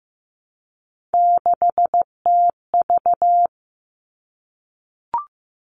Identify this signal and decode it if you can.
Morse code, heard as a keyed tone, about 15 wpm, 700 Hz. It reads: 6TV
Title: All Cut Numbers - 3-Digit 15wpm